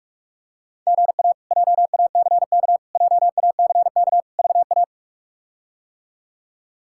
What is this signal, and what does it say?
Morse code, heard as a keyed tone, about 46 wpm, 700 Hz. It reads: GA JACK JACK VA
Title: K1USN SST QSO Practice - 45wpm